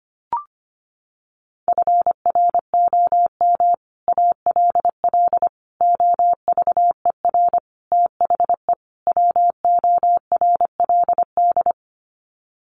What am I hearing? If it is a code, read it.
Morse code, heard as a keyed tone, about 25 wpm, 700 Hz. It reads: FROM ALL OVER THE WORLD
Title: Sets of 5 Words 25wpm